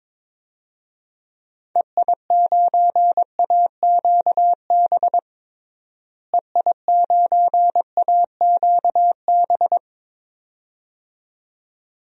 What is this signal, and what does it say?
Morse code, heard as a keyed tone, about 22 wpm, 700 Hz. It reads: EI9AQB EI9AQB
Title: DXCC-100 Call Signs - 6 Characters - Repeated 2x - 22wpm